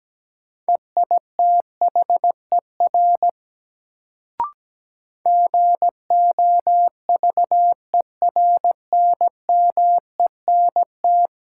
Morse code, heard as a keyed tone, about 17 wpm, 700 Hz. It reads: EITHER GOVERNMENT
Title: Sentences from Top 600 Words - New Word Drill - 17wpm